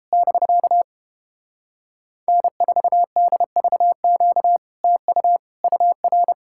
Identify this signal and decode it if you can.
Morse code, heard as a keyed tone, about 31 wpm, 700 Hz. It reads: <BK> N4DVQ TU UR